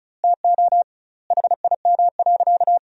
Morse code, heard as a keyed tone, about 36 wpm, 700 Hz. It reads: TO HIM.
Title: 2x Word-Spacing - Sentences from Top 400 Words - 35wpm